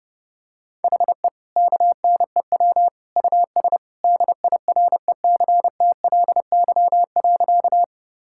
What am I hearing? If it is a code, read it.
Morse code, heard as a keyed tone, about 31 wpm, 700 Hz. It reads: HE KNEW US DIRECTLY.
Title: Sentences from Top 1000 Words - Review Part 5 - 30wpm